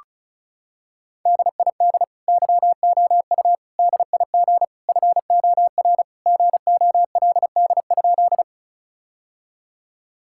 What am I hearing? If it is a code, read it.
Morse code, heard as a keyed tone, about 36 wpm, 700 Hz. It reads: DID YOU DIG FOR GOLD?
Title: Sentences from Top 1600 Words 35wpm